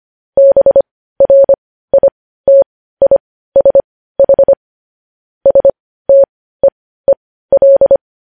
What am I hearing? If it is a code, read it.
Morse code, heard as a keyed tone, about 25 wpm, spaced as for 15 wpm, 550 Hz. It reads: BRITISH STEEL
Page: News Headlines in Morse Code at 15 WPM